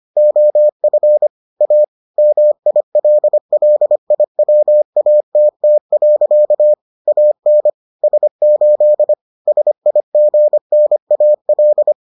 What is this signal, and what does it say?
Morse code, heard as a keyed tone, about 25 wpm, 600 Hz. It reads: OF A MILLIWATT. AN S8 SIGNAL